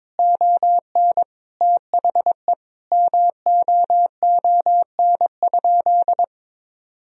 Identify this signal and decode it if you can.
Morse code, heard as a keyed tone, about 22 wpm, 700 Hz. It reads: ON THE MOON?
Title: Sentences from Top 1000 Words - Review Part 3 - 22wpm